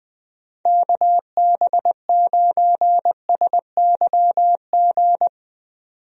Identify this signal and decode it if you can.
Morse code, heard as a keyed tone, about 20 wpm, 700 Hz. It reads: KB9SYG
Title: USA Call Signs - 6 Characters - 20wpm